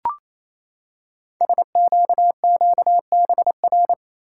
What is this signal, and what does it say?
Morse code, heard as a keyed tone, about 28 wpm, 700 Hz. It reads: SQQBR
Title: Code Groups - 5 Letters - 28wpm